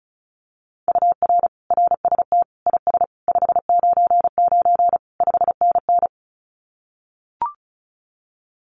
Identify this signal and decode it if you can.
Morse code, heard as a keyed tone, about 34 wpm, 700 Hz. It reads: UR RST IS 599 5NN
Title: Common QSO Phrases Encore 35wpm